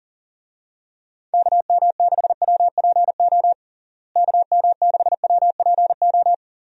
Morse code, heard as a keyed tone, about 40 wpm, 700 Hz. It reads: KM6WPO KM6WPO
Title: USA Call Signs - 6 Characters - Repeated 2x - 40wpm